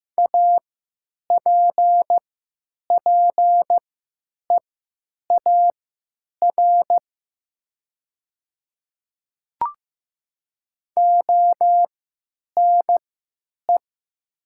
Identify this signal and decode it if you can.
Morse code, heard as a keyed tone, about 15 wpm, spaced as for 9 wpm, 700 Hz. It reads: APPEAR ONE
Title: 3x Character Spacing - Top 500 Words - 15wpm